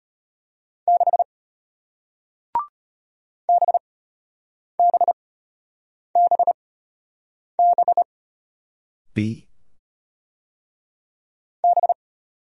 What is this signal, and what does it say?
Morse code, heard as a keyed tone, about 33 wpm, spaced as for 8 wpm, 700 Hz. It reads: 6 BBBB B